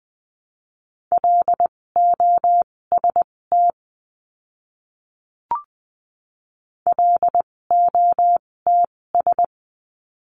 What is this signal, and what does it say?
Morse code, heard as a keyed tone, about 20 wpm, spaced as for 16 wpm, 700 Hz. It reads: LOST LOTS
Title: 256 Lesson 42a - Words 12-20wpm